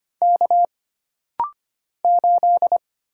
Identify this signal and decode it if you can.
Morse code, heard as a keyed tone, about 25 wpm, 700 Hz. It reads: K 8
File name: Single Character (Letter:Number:Punctuation:Prosign) - Rapid-Fire - 25wpm